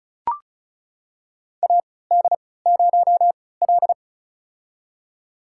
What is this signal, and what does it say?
Morse code, heard as a keyed tone, about 34 wpm, spaced as for 20 wpm, 700 Hz. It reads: AD0L
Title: USA Call Signs - 4 Characters - 3x Character Spacing - 35wpm